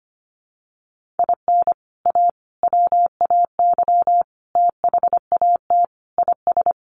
Morse code, heard as a keyed tone, about 25 wpm, 700 Hz. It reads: IN A WAY THAT IS
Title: Sets of 5 Words Encore 25wpm